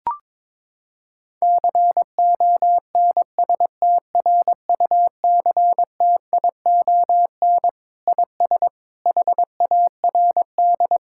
Morse code, heard as a keyed tone, about 22 wpm, 700 Hz. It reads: CONSTRUCTION IS HARD
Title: Sentences from Top 1600 Words 22wpm